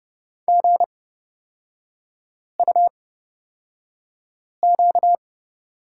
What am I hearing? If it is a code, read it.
Morse code, heard as a keyed tone, about 29 wpm, spaced as for 5 wpm, 700 Hz. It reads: GUQ
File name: Lesson 38e - Letter q - Three Characters 30wpm